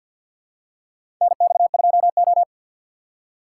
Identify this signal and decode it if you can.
Morse code, heard as a keyed tone, about 52 wpm, 700 Hz. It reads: NX2O
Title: USA Call Signs - 4 Characters - 6x Character Spacing - 50wpm